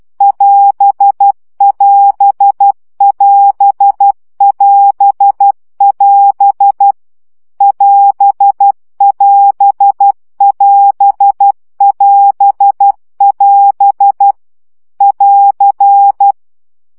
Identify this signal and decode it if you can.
Morse code, heard as a keyed tone, about 12 wpm, 800 Hz. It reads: <AS><AS><AS><AS><AS> <AS><AS><AS><AS><AS> <AR>
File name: CW gyakorlas as 10 baud